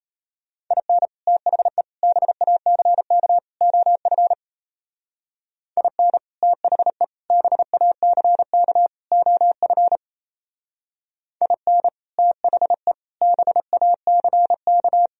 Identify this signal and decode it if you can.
Morse code, heard as a keyed tone, about 32 wpm, 700 Hz. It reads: IN THE BACK OF IN THE BACK OF IN THE BACK
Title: Speed-Racing - Sets of 4 Words 28wpm